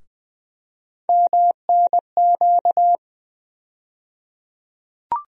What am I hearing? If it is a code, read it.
Morse code, heard as a keyed tone, about 20 wpm, 700 Hz. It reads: MNQ
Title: USA Call Sign Postfixes - 3 Letters 20wpm